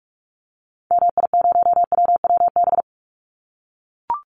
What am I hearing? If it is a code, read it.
Morse code, heard as a keyed tone, about 44 wpm, 700 Hz. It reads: MI0WWB
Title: Call Signs - 45wpm